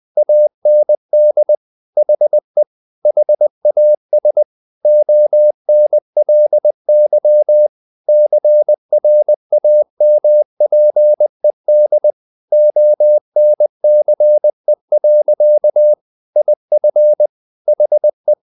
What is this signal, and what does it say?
Morse code, heard as a keyed tone, about 20 wpm, 600 Hz. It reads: AND HE HAS ONLY CRAMPED ONCE. IF HE
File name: the old man and the sea part 2 20wpm